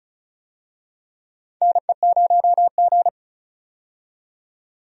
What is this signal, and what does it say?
Morse code, heard as a keyed tone, about 36 wpm, 700 Hz. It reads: NE0G
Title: USA Call Signs - 4 Characters - 35wpm